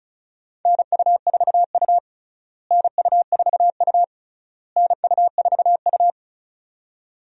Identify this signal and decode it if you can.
Morse code, heard as a keyed tone, about 36 wpm, 700 Hz. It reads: NU4U NU4U NU4U
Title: USA Call Signs - 4 Characters - Repeated 3x - 35wpm